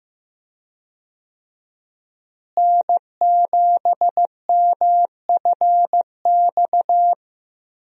Morse code, heard as a keyed tone, about 15 wpm, 700 Hz. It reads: N7MFX
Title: USA Call Signs - 5 Characters - Repeated 2x - 15wpm